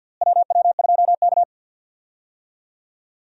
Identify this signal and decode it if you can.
Morse code, heard as a keyed tone, about 53 wpm, 700 Hz. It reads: WW2K
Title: USA Call Signs - 4 Characters - Repeated 3x - 50wpm